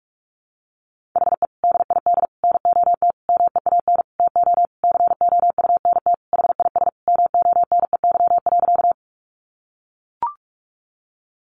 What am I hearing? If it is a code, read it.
Morse code, heard as a keyed tone, about 44 wpm, 700 Hz. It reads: HE DID NOT MEAN TO COUNT HIS MONEY.
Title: Sentences from Top 1500 Words 45wpm